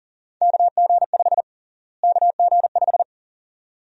Morse code, heard as a keyed tone, about 40 wpm, 700 Hz. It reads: KG5 KG5
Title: USA Call Sign Prefixes - Repeated 2x - 40wpm